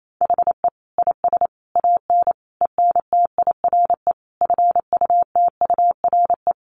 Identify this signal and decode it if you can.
Morse code, heard as a keyed tone, about 28 wpm, 700 Hz. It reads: HE IS AN ENTIRE FUTURE
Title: A Sound Of Thunder 28wpm